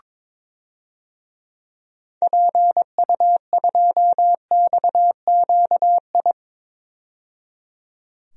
Morse code, heard as a keyed tone, about 21 wpm, 700 Hz. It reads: PU2XQI
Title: DXCC-100 Call Signs - 6 Characters - 22wpm